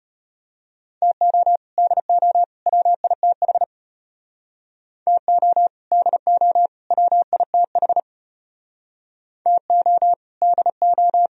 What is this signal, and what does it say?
Morse code, heard as a keyed tone, about 33 wpm, 700 Hz. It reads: TO DO WITH TO DO WITH TO DO